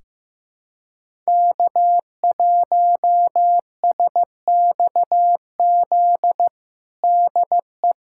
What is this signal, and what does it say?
Morse code, heard as a keyed tone, about 15 wpm, 700 Hz. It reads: K1SXZ DE
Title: USA Callsign DE Callsign - 5 Characters - 15wpm